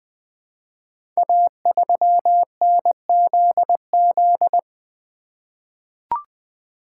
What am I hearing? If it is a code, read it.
Morse code, heard as a keyed tone, about 20 wpm, 700 Hz. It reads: A3NZZ
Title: Code Groups - 5 Letter:Numbers - 20wpm